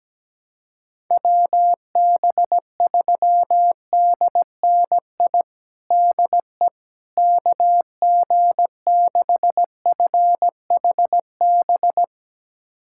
Morse code, heard as a keyed tone, about 17 wpm, 700 Hz. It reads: WB3DNI DE KG6FHB
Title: USA Callsign DE Callsign - 6 Characters - 17wpm